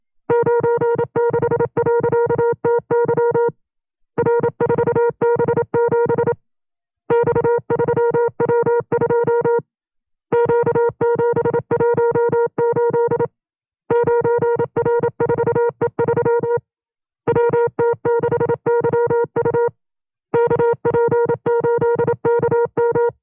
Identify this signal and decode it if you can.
Morse code, heard as a keyed tone, about 27 wpm, 470 Hz. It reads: 96.TY R4B7 X3W2 Q718 9R4E3 WT6YU KP8KM
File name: FreeFiles / CW-practice / 25WPM-side1